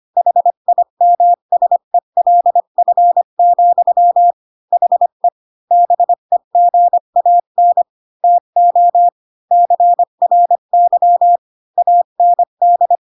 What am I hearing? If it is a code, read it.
Morse code, heard as a keyed tone, about 25 wpm, 700 Hz. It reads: HIMSELF, HE BEGAN TO CRY AND
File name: Pinocchio / Pinocchio ch5 20 25 ik1hge